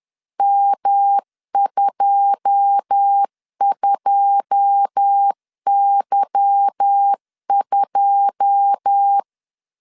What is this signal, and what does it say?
Morse code, heard as a keyed tone, about 10 wpm, 800 Hz. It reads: M22Y2
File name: LEZIONI / 9 2 10wpm 2736